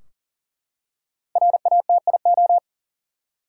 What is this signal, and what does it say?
Morse code, heard as a keyed tone, about 40 wpm, 700 Hz. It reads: RATIO
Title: Five Letter Words - 40wpm